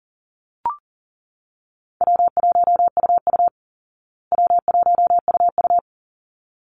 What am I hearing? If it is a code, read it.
Morse code, heard as a keyed tone, about 40 wpm, 700 Hz. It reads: W1UU W1UU
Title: Easy Call Signs Repeated 2x - 40wpm